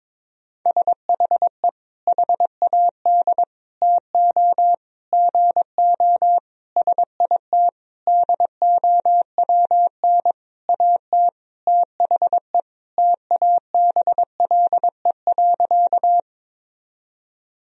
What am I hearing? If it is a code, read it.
Morse code, heard as a keyed tone, about 22 wpm, 700 Hz. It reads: SHE HAD TO GO SIT DOWN AT THE TABLE.
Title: Sentences from Top 1000 Words - Review Part 2 - 22wpm